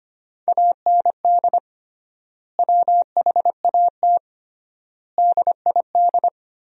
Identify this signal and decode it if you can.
Morse code, heard as a keyed tone, about 25 wpm, 700 Hz. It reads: AND WHAT DID
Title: 3x Word-Spacing - Sentences from Top 100 Words - 25wpm